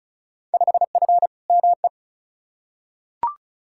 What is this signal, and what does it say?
Morse code, heard as a keyed tone, about 36 wpm, 700 Hz. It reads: 5F ME
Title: Field Day Exchanges Encore - 35wpm